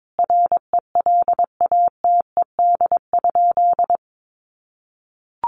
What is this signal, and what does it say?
Morse code, heard as a keyed tone, about 22 wpm, 700 Hz. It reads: RELATED?
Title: Sentences from Top 1400 Words 22wpm